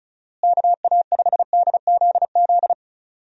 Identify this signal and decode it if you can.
Morse code, heard as a keyed tone, about 36 wpm, 700 Hz. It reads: KA5DZZ
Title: USA Call Signs - 6 Characters - Repeated 2x - 35wpm